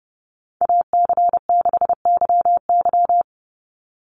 Morse code, heard as a keyed tone, about 30 wpm, 700 Hz. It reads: AC6YY
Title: Call Signs - 30wpm